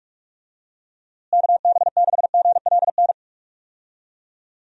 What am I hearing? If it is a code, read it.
Morse code, heard as a keyed tone, about 41 wpm, 700 Hz. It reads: KB6GLN
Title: USA Call Signs - 6 Characters - Repeated 3x - 45wpm